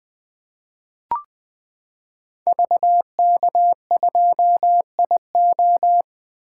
Morse code, heard as a keyed tone, about 20 wpm, 700 Hz. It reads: VK2IO
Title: 2023 POTA Activator Call Signs - 20wpm